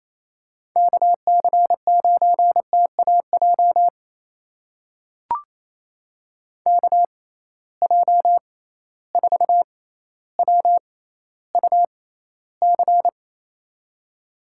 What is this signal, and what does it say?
Morse code, heard as a keyed tone, about 28 wpm, spaced as for 14 wpm, 700 Hz. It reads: KC9TAJ KJ4WUC
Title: USA Call Signs - 6 Characters - 6x Character Spacing - 28wpm